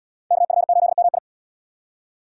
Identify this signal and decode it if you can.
Morse code, heard as a keyed tone, about 76 wpm, 700 Hz. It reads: KF8MI